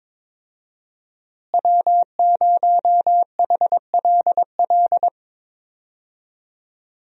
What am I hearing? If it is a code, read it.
Morse code, heard as a keyed tone, about 22 wpm, 700 Hz. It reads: W0HLL
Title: USA Call Signs - 5 Characters - Repeated 2x - 22wpm